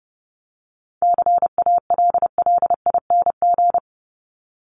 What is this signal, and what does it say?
Morse code, heard as a keyed tone, about 30 wpm, 700 Hz. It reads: CALLING
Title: ING Suffix - Top 100 Words with Base Words 30wpm